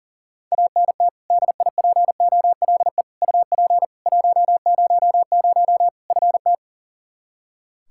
Morse code, heard as a keyed tone, about 40 wpm, 700 Hz. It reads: ANT DIPOLE UP 100 FT
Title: Practice QSO - Exchange 2 - Info - 40wpm